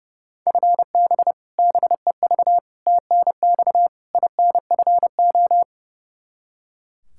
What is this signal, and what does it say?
Morse code, heard as a keyed tone, about 32 wpm, 700 Hz. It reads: FB BEV TNX INFO
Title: Practice QSO - Exchange 3 - More Info - 30wpm